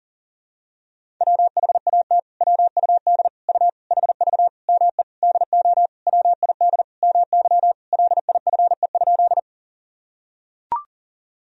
Morse code, heard as a keyed tone, about 40 wpm, 700 Hz. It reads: WHAT WUD U HV ME DO WID MY LIFE?
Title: Sentences from Top 300 Words and Abbreviations 40wpm